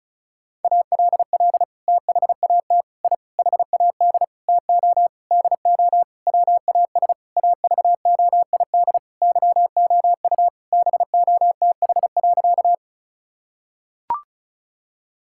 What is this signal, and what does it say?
Morse code, heard as a keyed tone, about 36 wpm, 700 Hz. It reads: ALL THAT I HAD TO DO WAS AVOID YOU BOTH.
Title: Sentences from Top 1000 Words - Review Part 6 - 35wpm